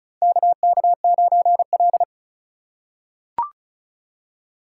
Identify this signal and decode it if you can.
Morse code, heard as a keyed tone, about 36 wpm, 700 Hz. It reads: KK9L
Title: USA Call Signs - 4 Characters - 4x Character Spacing - 35wpm